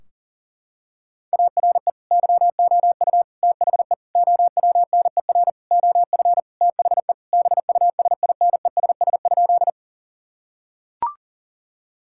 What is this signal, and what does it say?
Morse code, heard as a keyed tone, about 40 wpm, 700 Hz. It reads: ARE YOU THE OWNER OF THE BUSINESS?
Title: Sentences from Top 1600 Words 40wpm